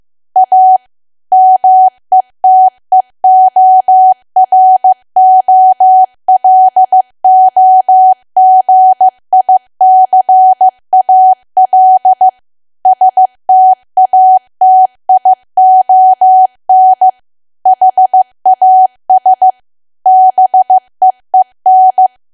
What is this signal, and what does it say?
Morse code, heard as a keyed tone, about 15 wpm, 750 Hz. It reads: A METEOROLOGICAL STATION HAS BEEN
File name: cwa / W1AW-Level2 / 15WPM